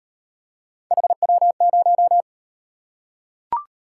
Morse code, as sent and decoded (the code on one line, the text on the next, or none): .... .-- -----
HW0